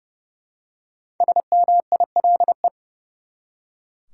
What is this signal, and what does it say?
Morse code, heard as a keyed tone, about 30 wpm, 700 Hz. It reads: SMILE
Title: Five Letter Words - 30wpm